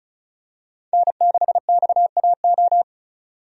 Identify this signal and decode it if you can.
Morse code, heard as a keyed tone, about 36 wpm, 700 Hz. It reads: N6XAO
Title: USA Call Signs - 5 Characters - 35wpm